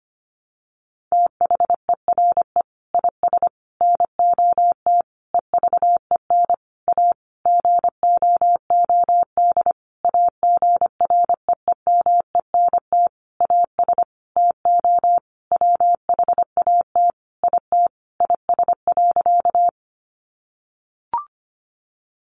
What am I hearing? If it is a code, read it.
Morse code, heard as a keyed tone, about 25 wpm, 700 Hz. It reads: THERE IS NOT EVEN A GOOD AGREEMENT AS TO WHAT IT IS.
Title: Sentences from Top 1500 Words 25wpm